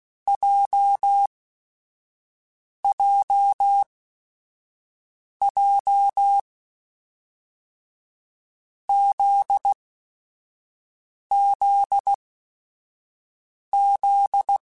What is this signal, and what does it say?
Morse code, heard as a keyed tone, about 16 wpm, spaced as for 5 wpm, 800 Hz. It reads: JJJ ZZZ
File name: Groups@5 / Groups@5 mp3s / JZ17@5